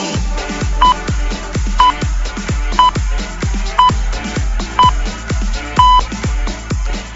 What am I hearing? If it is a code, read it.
Radio time signal (time pips). It is heard from 0.8 to 6.0 s.